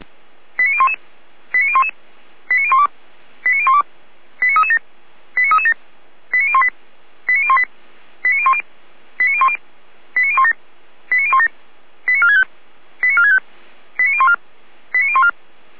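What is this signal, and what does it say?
Five-tone selective call (ZVEI), heard as quick runs of five tones, tones of about 70 ms. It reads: 89011 89011 89012 89012 89337 89337 89018 89018 89010 89010 89017 89017 89465 89465 89014 89014
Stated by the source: ZVEI five-tone selective call received on an SDR.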